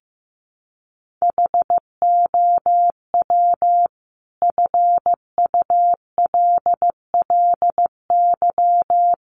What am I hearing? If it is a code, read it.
Morse code, heard as a keyed tone, about 15 wpm, 700 Hz. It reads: HOW FULLY
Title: Sentences from Top 1300 Words 15wpm